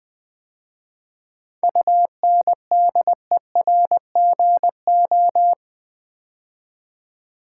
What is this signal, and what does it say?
Morse code, heard as a keyed tone, about 20 wpm, 690 Hz. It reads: UNDERGO
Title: 7 Letter Words - First 3 Letters Given - 20wpm